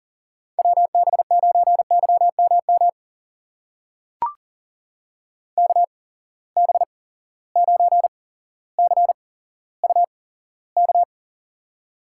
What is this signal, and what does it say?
Morse code, heard as a keyed tone, about 39 wpm, spaced as for 17 wpm, 700 Hz. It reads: WB9YMM KB9CUK
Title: USA Call Signs - 6 Characters - 8x Character Spacing - 40wpm